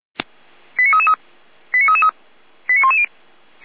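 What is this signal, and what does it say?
Five-tone selective call (ZVEI), heard as quick runs of five tones, tones of about 70 ms. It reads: 89333 89333 89110
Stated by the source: ZVEI five-tone selective call 89310 received by SDR.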